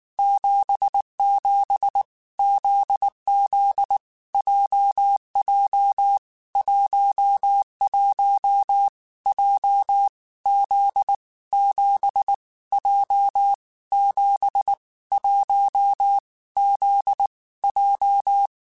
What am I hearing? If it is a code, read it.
Morse code, heard as a keyed tone, about 19 wpm, 800 Hz. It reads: 77 ZZ JJ 11 J Z 7 J 7 1 Z J
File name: Basic Sets / Basic Letter Groups Collective7 / JZ17@18